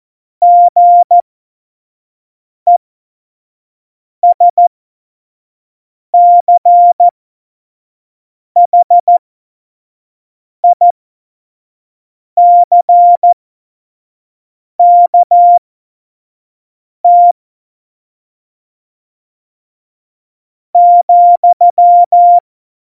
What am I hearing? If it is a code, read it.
Morse code, heard as a keyed tone, about 14 wpm, spaced as for 4 wpm, 700 Hz. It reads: GESCHICKT,
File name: Morsekurs / Witze / Witz 06 March 2018, 25 F70